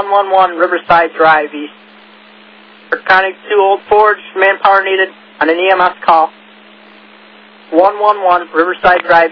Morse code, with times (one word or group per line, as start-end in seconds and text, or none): none